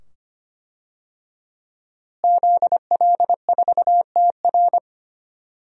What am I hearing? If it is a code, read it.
Morse code, heard as a keyed tone, about 25 wpm, 700 Hz. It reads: ZL4TR
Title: DXCC-100 Call Signs - 5 Characters - Repeated 2x - 25wpm